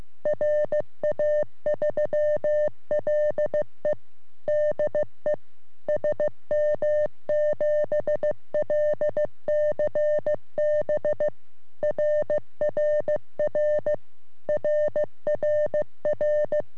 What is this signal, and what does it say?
Morse code, heard as a keyed tone, about 15 wpm, 600 Hz. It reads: RA3LE DE SM7LCB RRR RRR
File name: sm7lcb / ra3le 23cm